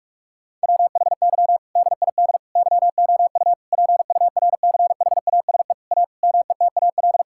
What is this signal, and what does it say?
Morse code, heard as a keyed tone, about 42 wpm, 700 Hz. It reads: WHY DID YOU PURCHASE A METAL
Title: Sentences from Top 1000 Words - Review Part 6 - 45wpm